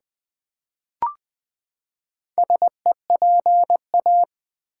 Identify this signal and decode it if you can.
Morse code, heard as a keyed tone, about 20 wpm, 700 Hz. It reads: SEPA
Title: Build A Word - 4 Letters At A Time - 10-Letter Words - 20wpm